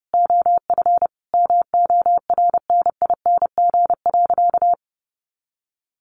Morse code, heard as a keyed tone, about 30 wpm, 700 Hz. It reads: OF MORNING.
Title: A Sound Of Thunder 30wpm